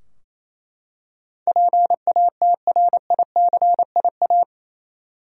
Morse code, heard as a keyed tone, about 27 wpm, 700 Hz. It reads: PATRICIA
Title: USA Common Names - 28wpm